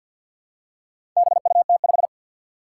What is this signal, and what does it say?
Morse code, heard as a keyed tone, about 48 wpm, 700 Hz. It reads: BUT5
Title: Four Challenging Letter-Number Combinations 50wpm